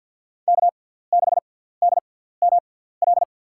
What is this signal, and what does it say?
Morse code, heard as a keyed tone, about 53 wpm, spaced as for 17 wpm, 700 Hz. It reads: K6DML
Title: USA Call Signs - 5 Characters - 6x Character Spacing - 50wpm